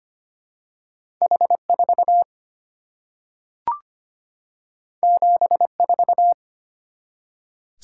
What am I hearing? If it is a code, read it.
Morse code, heard as a keyed tone, about 25 wpm, 700 Hz. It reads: H4 74
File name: Two Letter-Number Combinations - 25wpm